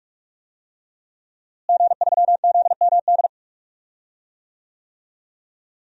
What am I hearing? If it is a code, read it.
Morse code, heard as a keyed tone, about 42 wpm, 700 Hz. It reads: G3ZMD
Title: DXCC-100 Call Signs - 5 Characters - Repeated 2x - 45wpm